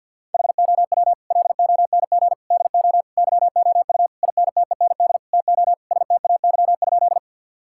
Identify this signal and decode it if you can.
Morse code, heard as a keyed tone, about 52 wpm, 700 Hz. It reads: HOW LONG DO YOU INTEND TO STAY?
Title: Sentences from Top 1600 Words 50wpm